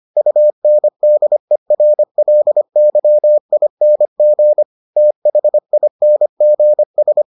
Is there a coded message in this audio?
Yes: Morse code from 0.2 to 7.2 s, heard as a keyed tone, about 25 wpm, 600 Hz. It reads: UNDERLYING THINGS